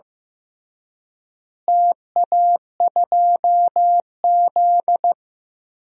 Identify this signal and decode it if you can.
Morse code, heard as a keyed tone, about 16 wpm, 700 Hz. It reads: TA2Z
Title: DXCC-100 Call Signs - 4 Characters - Repeated 3x - 15wpm